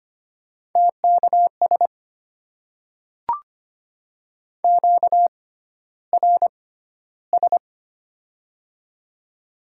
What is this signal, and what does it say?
Morse code, heard as a keyed tone, about 25 wpm, spaced as for 13 wpm, 700 Hz. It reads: TKS QRS
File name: Top 90 QSO Elements - 6x Character Spacing - 25wpm